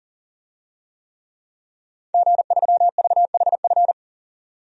Morse code, heard as a keyed tone, about 40 wpm, 700 Hz. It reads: G3VHF
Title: DXCC-100 Call Signs - 5 Characters - 40wpm